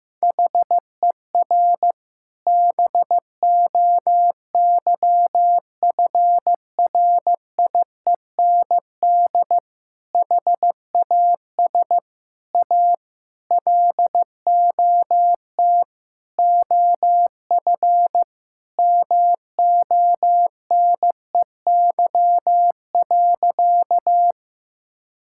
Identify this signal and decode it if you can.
Morse code, heard as a keyed tone, about 15 wpm, 700 Hz. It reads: HER BOYFRIEND HAS A LOT OF MONEY.
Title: Sentences from Top 1000 Words - Review Part 5 - 15wpm